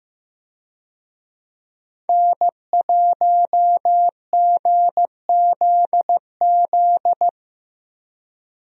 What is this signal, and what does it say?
Morse code, heard as a keyed tone, about 15 wpm, 700 Hz. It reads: N1GZZ
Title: USA Call Signs - 5 Characters - 15wpm